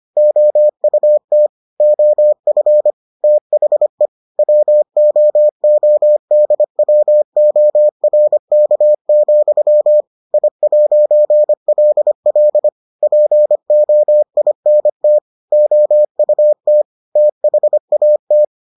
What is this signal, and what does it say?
Morse code, heard as a keyed tone, about 25 wpm, 600 Hz. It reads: OUT OF THE WOODWORK, I'LL POINT OUT THAT